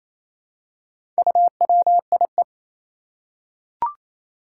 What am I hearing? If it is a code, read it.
Morse code, heard as a keyed tone, about 28 wpm, 700 Hz. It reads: UWIE
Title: Four Challenging Letter-Number Combinations 28wpm